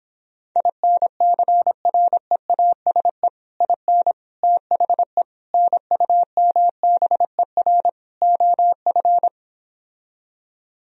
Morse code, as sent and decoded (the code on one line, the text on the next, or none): .. -. -.-. .-. .- ... / .. -. / - .... / -. ..- -- -... .-. / --- ..-.
INCREASE IN THE NUMBER OF